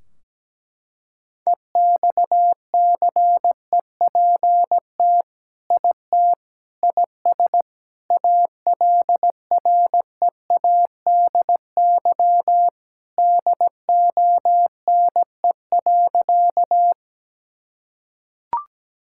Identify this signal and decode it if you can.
Morse code, heard as a keyed tone, about 17 wpm, 700 Hz. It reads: EXCEPT IT IS ALREADY DONE.
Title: Sentences from Top 1600 Words 17wpm